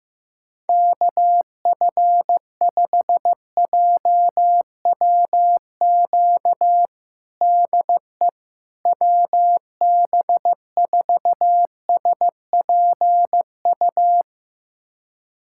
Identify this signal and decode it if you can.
Morse code, heard as a keyed tone, about 15 wpm, 700 Hz. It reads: KF5JWQ DE WB4SPU
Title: USA Callsign DE Callsign - 6 Characters - 15wpm